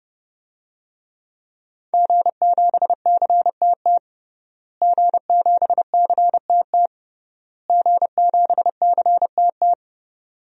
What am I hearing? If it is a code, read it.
Morse code, heard as a keyed tone, about 30 wpm, 700 Hz. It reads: G7CTT G7CTT G7CTT
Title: DXCC-100 Call Signs - 5 Characters - Repeated 3x - 30wpm